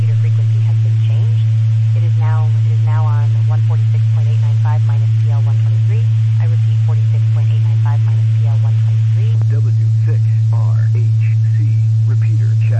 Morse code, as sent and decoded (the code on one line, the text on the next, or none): none